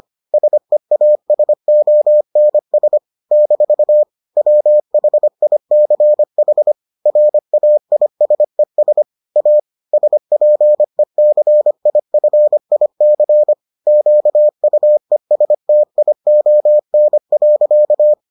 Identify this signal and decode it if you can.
Morse code, heard as a keyed tone, about 25 wpm, 600 Hz. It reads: SEASONS - WHICH RAISES A SPECIFIC QUESTION.